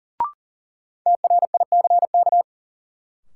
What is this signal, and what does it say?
Morse code, heard as a keyed tone, about 40 wpm, 700 Hz. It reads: TRICK
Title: Sentences from Top 1500 Words - New Word Drill - 40wpm